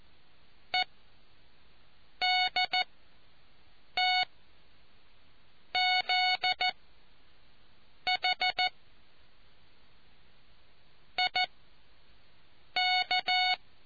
Morse code, heard as a keyed tone, about 14 wpm, spaced as for 5 wpm, 740 Hz. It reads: EDTZH IK